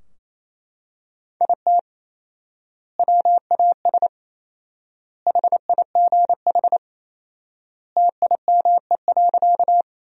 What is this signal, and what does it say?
Morse code, heard as a keyed tone, about 28 wpm, 700 Hz. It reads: IT WAS HIGH TIME.
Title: 4x Word-Spacing - Sentences from Top 200 Words - 28wpm